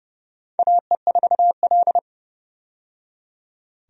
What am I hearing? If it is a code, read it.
Morse code, heard as a keyed tone, about 32 wpm, 700 Hz. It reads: AE4L
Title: USA Call Signs - 4 Characters - 30wpm